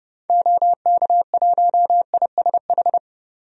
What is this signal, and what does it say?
Morse code, heard as a keyed tone, about 32 wpm, 700 Hz. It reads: OK1ISH
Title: DXCC-100 Call Signs - 6 Characters - 30wpm